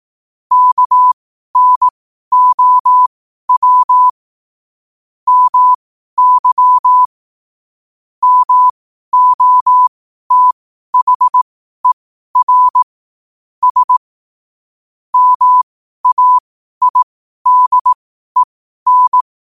Morse code, heard as a keyed tone, about 18 wpm, spaced as for 12 wpm, 1000 Hz. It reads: KNOW MY MOTHER S MAIDEN
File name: qotdfiles / 12wpm